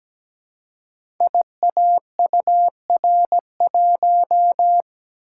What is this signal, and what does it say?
Morse code, heard as a keyed tone, about 17 wpm, 700 Hz. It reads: IAUR1
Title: Code Groups - 5 Letter:Numbers - 17wpm